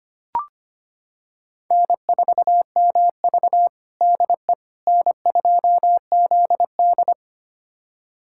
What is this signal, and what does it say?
Morse code, heard as a keyed tone, about 25 wpm, 700 Hz. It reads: N4MV DE N2ZD
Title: USA Callsign DE Callsign - 4 Characters - 25wpm